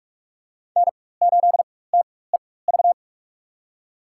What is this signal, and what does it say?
Morse code, heard as a keyed tone, about 46 wpm, spaced as for 21 wpm, 700 Hz. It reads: N8TEV
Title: USA Call Signs - 5 Characters - 4x Character Spacing - 45wpm